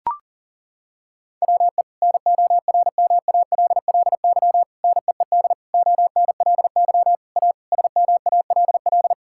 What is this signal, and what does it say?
Morse code, heard as a keyed tone, about 40 wpm, 700 Hz. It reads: WE NORMALLY NEED ONLY A SMALL
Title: Sentences from Top 1000 Words - Review Part 6 - 40wpm